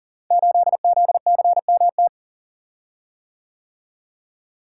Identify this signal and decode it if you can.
Morse code, heard as a keyed tone, about 39 wpm, 700 Hz. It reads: 8ZCMT
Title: Code Groups - 5 Letter:Numbers - 40wpm